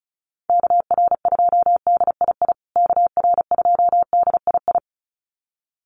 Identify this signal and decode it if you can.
Morse code, heard as a keyed tone, about 34 wpm, 700 Hz. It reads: KR2DII KR2DII
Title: USA Call Signs Repeated 2x 35wpm